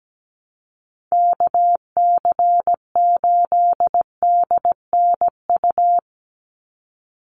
Morse code, heard as a keyed tone, about 17 wpm, 700 Hz. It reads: KC8DNU
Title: USA Call Signs 17wpm